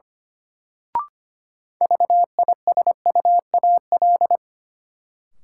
Morse code, heard as a keyed tone, about 25 wpm, 700 Hz. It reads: VISUAL
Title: Sets of 5 Words Encore - New Word Drill - 25wpm